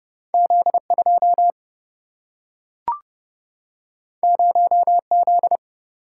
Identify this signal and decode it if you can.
Morse code, heard as a keyed tone, about 30 wpm, 700 Hz. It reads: Z2 0Z
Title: Single-Letter Single-Number 30wpm